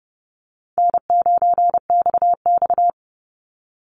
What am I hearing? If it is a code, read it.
Morse code, heard as a keyed tone, about 30 wpm, 700 Hz. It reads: N9XX
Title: Call Signs Repeated 2x - 30wpm